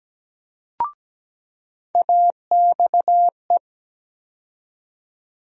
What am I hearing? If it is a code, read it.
Morse code, heard as a keyed tone, about 17 wpm, 700 Hz. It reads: AXE
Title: USA Call Sign Postfixes - 3 Letters 17wpm